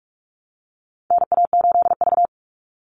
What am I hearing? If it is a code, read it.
Morse code, heard as a keyed tone, about 44 wpm, 700 Hz. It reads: NA8V
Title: Call Signs - 45wpm